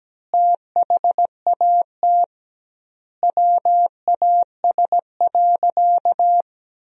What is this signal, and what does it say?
Morse code, heard as a keyed tone, about 17 wpm, 700 Hz. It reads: THAT WAS.
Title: 2x Word-Spacing - Sentences from Top 300 Words - 17wpm